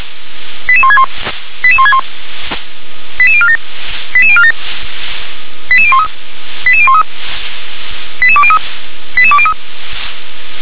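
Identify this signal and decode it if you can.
Five-tone selective call (ZVEI), heard as quick runs of five tones, tones of about 70 ms. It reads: 88161 88161 88047 88047 88013 88013 88203 88203
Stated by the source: ZVEI five-tone selective call received on SDR.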